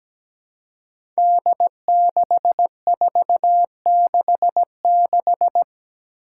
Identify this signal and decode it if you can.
Morse code, heard as a keyed tone, about 17 wpm, 700 Hz. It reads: D6466
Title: Five Challenging Letter-Number Combinations 17wpm